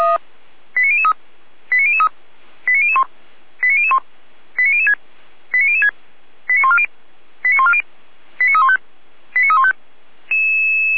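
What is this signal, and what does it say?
Five-tone selective call (ZVEI), heard as quick runs of five tones, tones of about 70 ms. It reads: 89003 89003 89001 89001 89006 89006 89140 89140 89315 89315
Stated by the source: ZVEI five-tone call received on SDR, from feuerwehr-orlamuende fire service server.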